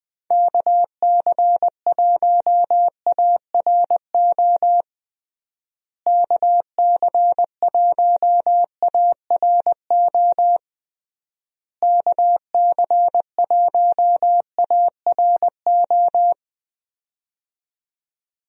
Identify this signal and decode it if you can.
Morse code, heard as a keyed tone, about 20 wpm, 700 Hz. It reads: KC1ARO KC1ARO KC1ARO
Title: USA Call Signs - 6 Characters - Repeated 3x - 20wpm